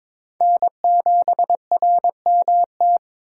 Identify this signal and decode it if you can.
Morse code, heard as a keyed tone, about 22 wpm, 700 Hz. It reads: N7RMT
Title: USA Call Signs - 5 Characters - 3x Character Spacing - 22wpm